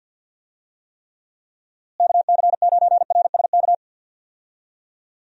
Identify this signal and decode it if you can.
Morse code, heard as a keyed tone, about 53 wpm, 700 Hz. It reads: KC9RSK
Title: USA Call Signs - 6 Characters - Repeated 2x - 50wpm